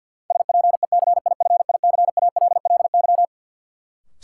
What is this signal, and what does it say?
Morse code, heard as a keyed tone, about 50 wpm, 700 Hz. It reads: SPECIFICALLY
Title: Sentences from Top 1100 Words - New Word Drill Alt - 50wpm